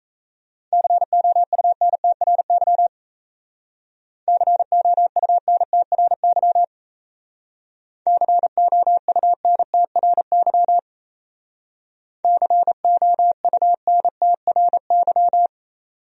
Morse code, heard as a keyed tone, about 34 wpm, 700 Hz. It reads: COUNTRY COUNTRY COUNTRY COUNTRY
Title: Speed-Racing Top 300 Words 28wpm